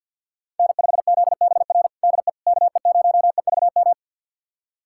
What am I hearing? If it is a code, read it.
Morse code, heard as a keyed tone, about 51 wpm, 700 Hz. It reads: N5ZBR DE KE0EVM
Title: USA Callsign DE Callsign 50wpm